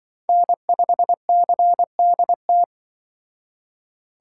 Morse code, heard as a keyed tone, about 25 wpm, 700 Hz. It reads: N5CDT
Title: Speed Racing - USA Call Signs 20wpm